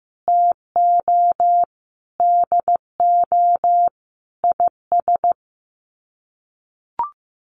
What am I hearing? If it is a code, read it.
Morse code, heard as a keyed tone, about 15 wpm, 700 Hz. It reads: TO DO IS
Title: Sets of 5 Words 15wpm